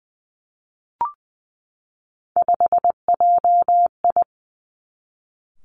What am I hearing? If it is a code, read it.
Morse code, heard as a keyed tone, about 19 wpm, 700 Hz. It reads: 5JI
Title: Three Challenging Letter-Number Combinations 20wpm